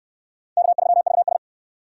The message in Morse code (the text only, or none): K3VS